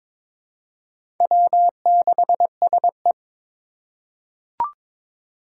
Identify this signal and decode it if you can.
Morse code, heard as a keyed tone, about 21 wpm, 700 Hz. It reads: W6SE
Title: USA Call Signs - 4 Characters - 3x Character Spacing - 22wpm